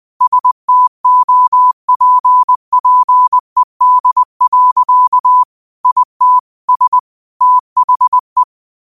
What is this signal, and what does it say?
Morse code, heard as a keyed tone, about 20 wpm, 1000 Hz. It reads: STOPPED. IT S THE